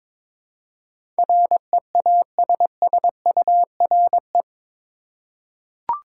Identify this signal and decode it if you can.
Morse code, heard as a keyed tone, about 22 wpm, 700 Hz. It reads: REASSURE
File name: practice / 22wpm / Eight Letter Words - 22wpm